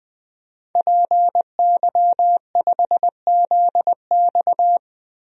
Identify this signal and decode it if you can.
Morse code, heard as a keyed tone, about 21 wpm, 700 Hz. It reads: PY5ZX
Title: DXCC-100 Call Signs - 5 Characters - 20wpm